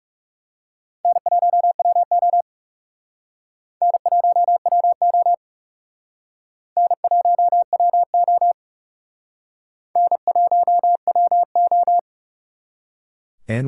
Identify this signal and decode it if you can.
Morse code, heard as a keyed tone, about 36 wpm, 700 Hz. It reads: N1WO N1WO N1WO N1WO
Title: Speed Racing - USA 1x2 Call Signs 30wpm